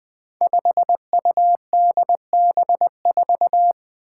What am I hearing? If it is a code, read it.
Morse code, heard as a keyed tone, about 20 wpm, 700 Hz. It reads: 5UDB4
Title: Five Challenging Letter-Number Combinations 20wpm